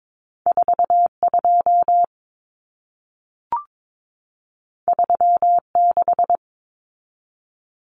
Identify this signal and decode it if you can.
Morse code, heard as a keyed tone, about 22 wpm, 700 Hz. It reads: 42 36
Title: Common Cut Numbers - 2-Digit 22wpm